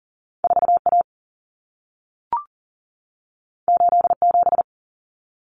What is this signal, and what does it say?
Morse code, heard as a keyed tone, about 40 wpm, 700 Hz. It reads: 4A 87
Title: Common Cut Numbers - 2-Digit 40wpm